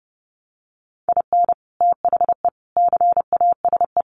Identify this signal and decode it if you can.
Morse code, heard as a keyed tone, about 30 wpm, 700 Hz. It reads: IN THE CASE
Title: Sets of 3 Words 30wpm